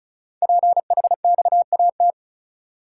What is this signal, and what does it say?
Morse code, heard as a keyed tone, about 36 wpm, 700 Hz. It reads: PHXAT
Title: Code Groups - 5 Letter:Numbers - 35wpm